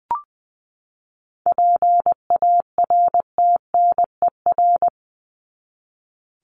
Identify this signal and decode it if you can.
Morse code, heard as a keyed tone, about 20 wpm, 700 Hz. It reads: PARTNER
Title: Top 1501-1750 Words 20wpm